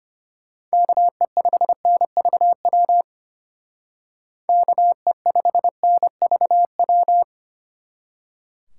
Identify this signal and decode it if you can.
Morse code, heard as a keyed tone, about 33 wpm, 700 Hz. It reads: KE5NVW KE5NVW